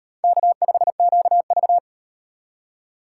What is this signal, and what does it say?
Morse code, heard as a keyed tone, about 34 wpm, 700 Hz. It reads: K5QV